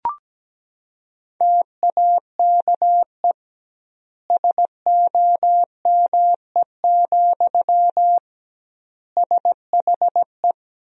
Morse code, heard as a keyed tone, about 17 wpm, 700 Hz. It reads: TAKE SOME, SHE
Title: 2x Word-Spacing - Sentences from Top 200 Words - 17wpm